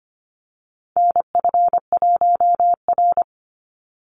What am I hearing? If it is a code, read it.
Morse code, heard as a keyed tone, about 25 wpm, 700 Hz. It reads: NF1R
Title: NA SOTA Chaser Callsigns 25wpm